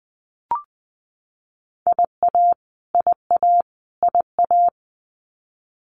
Morse code, heard as a keyed tone, about 20 wpm, 700 Hz. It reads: IA IA IA
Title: US States Repeated 3x 20wpm